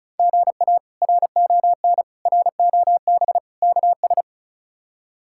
Morse code, heard as a keyed tone, about 36 wpm, 700 Hz. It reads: GA RON ROB KS
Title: K1USN SST QSO Practice - 35wpm